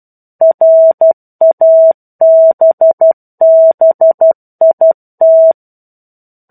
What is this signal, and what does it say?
Morse code, heard as a keyed tone, about 12 wpm, 650 Hz. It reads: RABBIT
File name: 01-Chapter-01 12 WPM